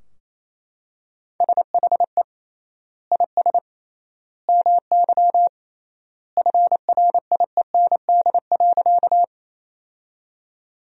Morse code, heard as a keyed tone, about 28 wpm, 700 Hz. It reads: SHE IS MY FRIEND.
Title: 3x Word-Spacing - Sentences from Top 300 Words - 28wpm